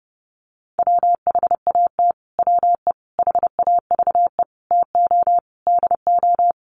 Morse code, heard as a keyed tone, about 30 wpm, 700 Hz. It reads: WHAT WE HAVE TO DO
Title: Sets of 5 Words Encore 30wpm